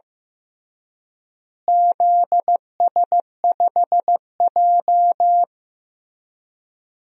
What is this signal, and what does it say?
Morse code, heard as a keyed tone, about 15 wpm, 700 Hz. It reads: ZS5J
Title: DXCC-100 Call Signs - 4 Characters - Repeated 3x - 15wpm